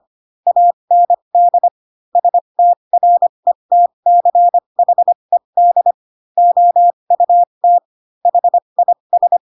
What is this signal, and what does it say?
Morse code, heard as a keyed tone, about 25 wpm, spaced as for 22 wpm, 700 Hz. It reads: AND STRETCHED OUT HIS